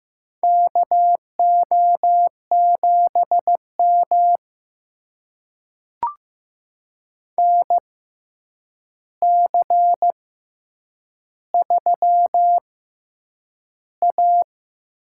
Morse code, heard as a keyed tone, about 15 wpm, spaced as for 8 wpm, 700 Hz. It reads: KO7M NC3A
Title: USA Call Signs - 4 Characters - 6x Character Spacing - 15wpm